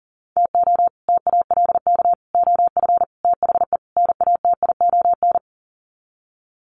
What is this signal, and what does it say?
Morse code, heard as a keyed tone, about 40 wpm, 700 Hz. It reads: TO TALK OF THE NATION
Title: Sets of 5 Words 40wpm